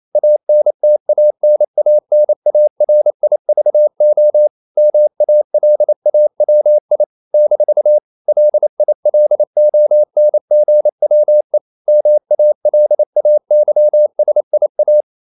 Morse code, heard as a keyed tone, about 28 wpm, 600 Hz. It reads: ANTANANARIVO MALAWI - LILONGWE MALAYSIA
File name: CW / Practice / NationalCapitals28